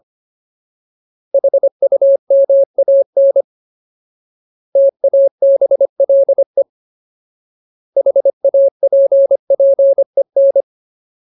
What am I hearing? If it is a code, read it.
Morse code, heard as a keyed tone, about 25 wpm, 550 Hz. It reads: HUMAN TABLE HAPPEN
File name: Ultimate Word List / 25WPM Ultimate